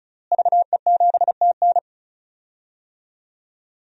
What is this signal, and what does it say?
Morse code, heard as a keyed tone, about 34 wpm, 700 Hz. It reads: VE7TN
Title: DXCC-100 Call Signs - 5 Characters - Repeated 3x - 35wpm